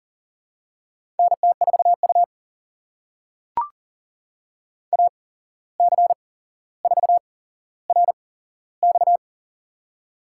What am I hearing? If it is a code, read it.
Morse code, heard as a keyed tone, about 40 wpm, spaced as for 16 wpm, 700 Hz. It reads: NT4U AC4RX
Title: USA Call Signs - 8x Character Spacing - 40wpm